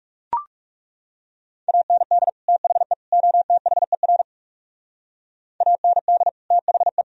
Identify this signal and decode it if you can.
Morse code, heard as a keyed tone, about 38 wpm, 700 Hz. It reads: AND THE OTHER AND THE